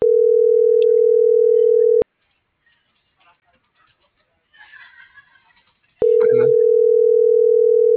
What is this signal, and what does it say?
Ringback tone (North America), 440 + 480 Hz.